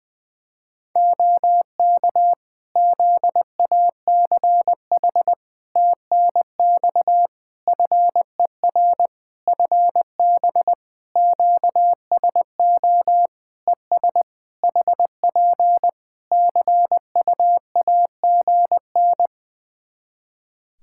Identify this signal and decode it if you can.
Morse code, heard as a keyed tone, about 21 wpm, 700 Hz. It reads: OK ZACH TNX FER FB QSO ES HP CUAGN
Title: K1USN SST QSO Practice 20wpm